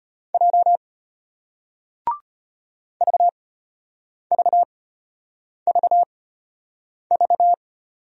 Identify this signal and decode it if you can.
Morse code, heard as a keyed tone, about 34 wpm, spaced as for 8 wpm, 700 Hz. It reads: J VVVV